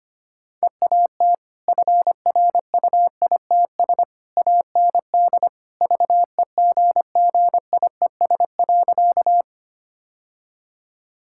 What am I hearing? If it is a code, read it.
Morse code, heard as a keyed tone, about 25 wpm, 700 Hz. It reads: EAT FRUITS AND VEGGIES.
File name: Easy Sentences with Hints - 25wpm